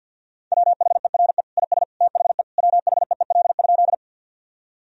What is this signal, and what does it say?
Morse code, heard as a keyed tone, about 50 wpm, 700 Hz. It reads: WHERE IS THE WHEEL?
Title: Sentences from Top 1000 Words - Review Part 3 - 50wpm